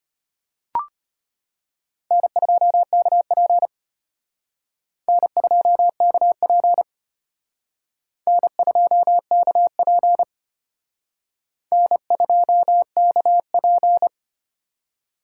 Morse code, heard as a keyed tone, about 31 wpm, 700 Hz. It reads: N2KP N2KP N2KP N2KP